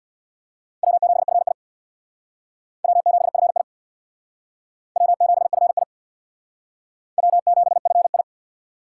Morse code, heard as a keyed tone, about 64 wpm, 700 Hz. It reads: W7FI W7FI W7FI W7FI